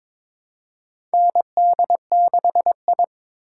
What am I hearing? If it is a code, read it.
Morse code, heard as a keyed tone, about 22 wpm, 700 Hz. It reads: ND6I
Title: USA Call Signs - 4 Characters - 3x Character Spacing - 22wpm